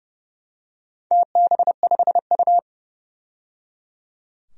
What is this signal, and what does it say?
Morse code, heard as a keyed tone, about 30 wpm, 700 Hz. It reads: TB5U
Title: DXCC-100 Call Signs - 4 Characters - 30wpm